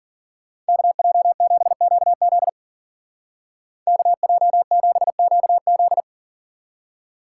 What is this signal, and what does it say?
Morse code, heard as a keyed tone, about 52 wpm, 700 Hz. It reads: KJ7QZ KJ7QZ